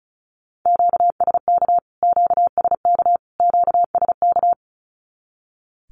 Morse code, heard as a keyed tone, about 34 wpm, 700 Hz. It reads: QSK QSK QSK
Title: Q-Codes Repeated 3x 35wpm